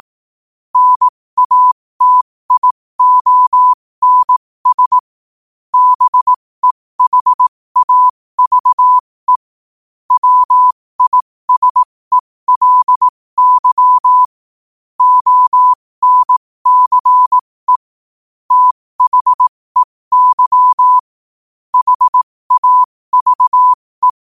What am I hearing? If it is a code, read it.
Morse code, heard as a keyed tone, about 18 wpm, spaced as for 15 wpm, 1000 Hz. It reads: NATIONS BEHAVE WISELY ONCE THEY HAVE
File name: qotdfiles / 15wpm